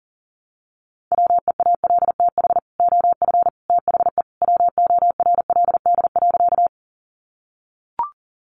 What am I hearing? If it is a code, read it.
Morse code, heard as a keyed tone, about 37 wpm, 700 Hz. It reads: WEALTH OF THE WORLD.
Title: Sentences from Top 1200 Words 40wpm